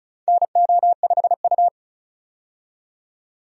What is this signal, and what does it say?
Morse code, heard as a keyed tone, about 34 wpm, 700 Hz. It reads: NO5U
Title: USA Call Signs - 4 Characters - 35wpm